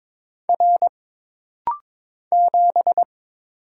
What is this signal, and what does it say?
Morse code, heard as a keyed tone, about 22 wpm, 700 Hz. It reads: R 7
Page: This rapid-fire timing is geared to push your ICR (Instant Character Recognition) skills.